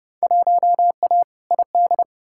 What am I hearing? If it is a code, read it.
Morse code, heard as a keyed tone, about 30 wpm, 700 Hz. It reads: 1A ID
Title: Field Day Exchanges - 30wpm